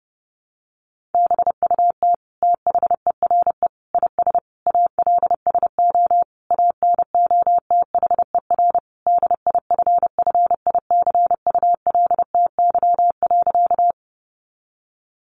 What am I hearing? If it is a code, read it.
Morse code, heard as a keyed tone, about 30 wpm, 700 Hz. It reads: BUT THERE IS ALSO ANOTHER DIFFICULTY.
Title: Sentences from Top 1500 Words 30wpm